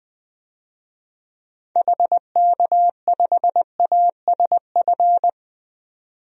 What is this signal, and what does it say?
Morse code, heard as a keyed tone, about 21 wpm, 700 Hz. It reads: HK5ASF
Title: DXCC-100 Call Signs - 6 Characters - 20wpm